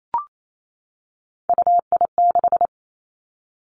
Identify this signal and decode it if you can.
Morse code, heard as a keyed tone, about 28 wpm, 700 Hz. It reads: UI6
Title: Three Challenging Letter-Number Combinations 28wpm